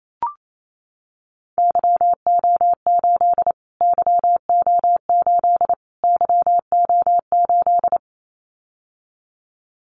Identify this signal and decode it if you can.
Morse code, heard as a keyed tone, about 28 wpm, 700 Hz. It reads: YO8 YO8 YO8
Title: Callsign Prefixes for Top 25 DXCC - Repeated 3x - 28wpm